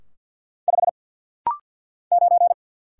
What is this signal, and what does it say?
Morse code, heard as a keyed tone, about 50 wpm, 700 Hz. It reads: 5 9
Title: Single Letter-Number - Rapid-Fire 50wpm